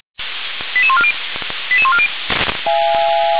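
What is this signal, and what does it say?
Five-tone selective call (ZVEI), heard as quick runs of five tones, tones of about 70 ms. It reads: 88140 88140
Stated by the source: ZVEI five-tone selective call 88149 received by SDR, from fire brigade Orlamünde (feuerwehr-orlamuende) server.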